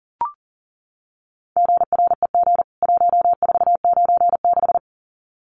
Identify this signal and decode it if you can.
Morse code, heard as a keyed tone, about 40 wpm, 700 Hz. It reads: GREG 1496
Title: CWT Exchange 40wpm